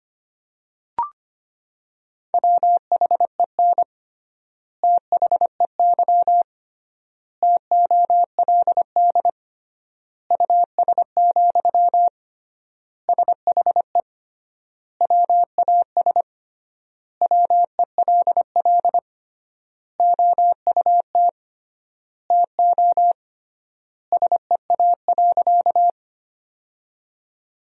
Morse code, heard as a keyed tone, about 25 wpm, 700 Hz. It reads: WHEN THEY TOLD US, SHE WAS WELL OUT TO SEA.
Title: 3x Word-Spacing - Sentences from Top 400 Words - 25wpm